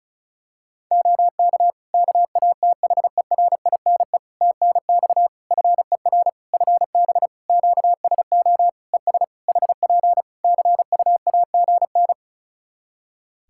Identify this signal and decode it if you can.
Morse code, heard as a keyed tone, about 34 wpm, 700 Hz. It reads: OK KATHERINE TNX FER FB QSO ES HP CUAGN
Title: K1USN SST QSO Practice 35wpm